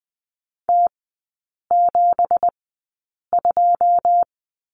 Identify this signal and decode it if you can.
Morse code, heard as a keyed tone, about 20 wpm, spaced as for 8 wpm, 700 Hz. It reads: T72
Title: 166 Lesson 27e - Number 7 - Three Characters 20wpm